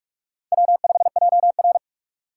Morse code, heard as a keyed tone, about 43 wpm, 700 Hz. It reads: W5JR
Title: USA Call Signs - 4 Characters - 6x Character Spacing - 45wpm